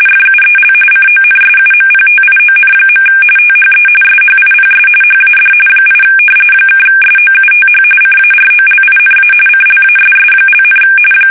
RTTY, about 45.45 baud. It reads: FUEL CONSUMPTION. THE SHIPS HAD 8 (SOUTH CAROLINA CLASS),
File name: fsk850 / battleships